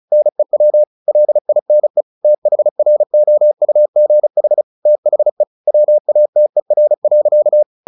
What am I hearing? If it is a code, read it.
Morse code, heard as a keyed tone, about 34 wpm, 600 Hz. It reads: NEW LINE THROUGH THE WATER.
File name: the old man and the sea 2 35wpm